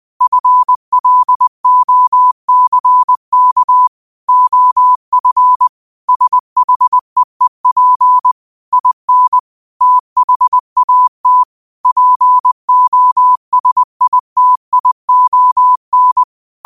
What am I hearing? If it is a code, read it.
Morse code, heard as a keyed tone, about 20 wpm, 1000 Hz. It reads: FLOCK OF SHEEP IN THAT POSITION
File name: qotdfiles / 20wpm